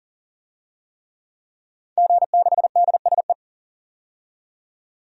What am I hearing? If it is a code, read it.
Morse code, heard as a keyed tone, about 40 wpm, 700 Hz. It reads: G6DSE
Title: DXCC-100 Call Signs - 5 Characters - 40wpm